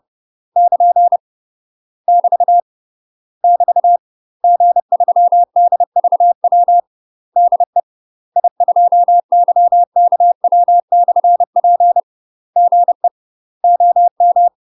Morse code, heard as a keyed tone, about 30 wpm, 700 Hz. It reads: <KN> <BT> <BT> G3DVW DE I2YKW/P GE OM
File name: UK MASTER QSOs TRAINING/30WPM